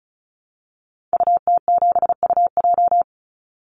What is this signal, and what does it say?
Morse code, heard as a keyed tone, about 34 wpm, 700 Hz. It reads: UT7UJ
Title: Call Signs Repeated 2x - 35wpm